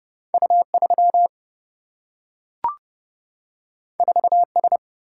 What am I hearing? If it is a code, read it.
Morse code, heard as a keyed tone, about 31 wpm, 700 Hz. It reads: U 3 4 S
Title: Single-Letter Single-Number 30wpm